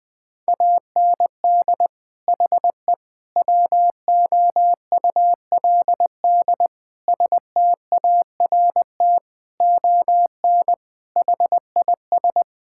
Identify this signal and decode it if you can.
Morse code, heard as a keyed tone, about 20 wpm, 700 Hz. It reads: AND HE WOULD START ON HIS
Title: Sentences from Top 1600 Words 20wpm